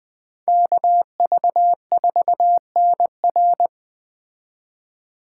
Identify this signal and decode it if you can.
Morse code, heard as a keyed tone, about 21 wpm, 700 Hz. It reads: KV4NR